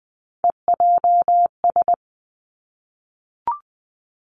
Morse code, heard as a keyed tone, about 20 wpm, 700 Hz. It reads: EJS
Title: Three Challenging Letter-Number Combinations 20wpm